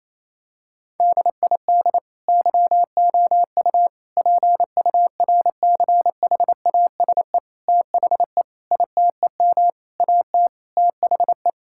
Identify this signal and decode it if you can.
Morse code, heard as a keyed tone, about 28 wpm, 700 Hz. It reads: DID YOU PURCHASE THE ITEM AT THE
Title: Sentences from Top 1000 Words - Review Part 6 - 28wpm